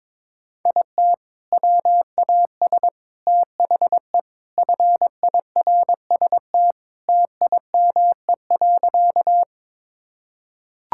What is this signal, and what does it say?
Morse code, heard as a keyed tone, about 22 wpm, 700 Hz. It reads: IT WAS THE FIRST TIME.
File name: practice / 22wpm / Sentences with 4-Word Phrases - 22wpm